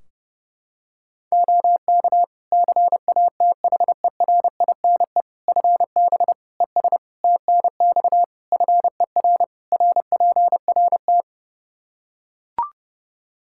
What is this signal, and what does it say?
Morse code, heard as a keyed tone, about 30 wpm, 700 Hz. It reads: OK CATHERINE FB ES TNX FER RPRT
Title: Practice QSO - Exchange 2 - Info - 30wpm